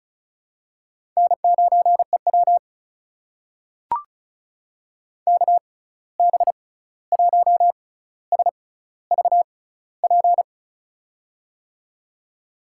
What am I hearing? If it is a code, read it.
Morse code, heard as a keyed tone, about 36 wpm, spaced as for 16 wpm, 700 Hz. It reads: N9EW KB1SVP
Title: USA Call Signs - 6x Character Spacing - 35wpm